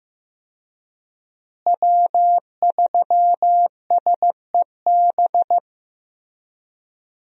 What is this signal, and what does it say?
Morse code, heard as a keyed tone, about 15 wpm, 700 Hz. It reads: W3SEB
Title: USA Call Signs - 5 Characters - Repeated 2x - 15wpm